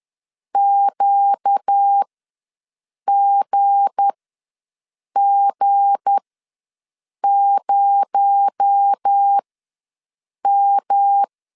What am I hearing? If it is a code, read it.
Morse code, heard as a keyed tone, about 10 wpm, spaced as for 6 wpm, 800 Hz. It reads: QGG0M
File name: LEZIONI / 7 GF9Z / 7 G ws 10wpm 3843